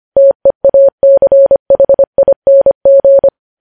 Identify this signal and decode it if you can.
Morse code, heard as a keyed tone, about 25 wpm, 550 Hz. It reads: TEACHING
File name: podcast / News headlines-1x-25x25